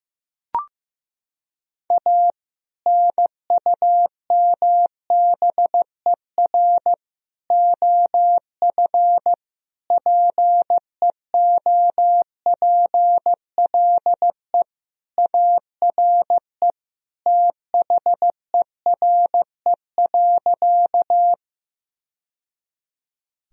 Morse code, heard as a keyed tone, about 15 wpm, 700 Hz. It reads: A NUMBER OF PEOPLE ARE THERE.
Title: Sentences from Top 1000 Words - Review Part 1 - 15wpm